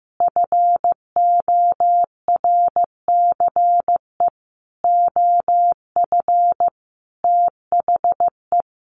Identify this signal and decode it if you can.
Morse code, heard as a keyed tone, about 15 wpm, 700 Hz. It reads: FORCE OF THE
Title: Sentences from Top 1400 Words 15wpm